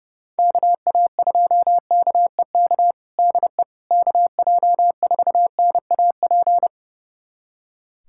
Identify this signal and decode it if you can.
Morse code, heard as a keyed tone, about 29 wpm, 700 Hz. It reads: KA2KEK DE KJ4NAP
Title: USA Callsign DE Callsign - 6 Characters - 30wpm